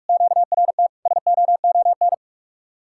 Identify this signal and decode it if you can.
Morse code, heard as a keyed tone, about 49 wpm, 700 Hz. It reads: QRT SOON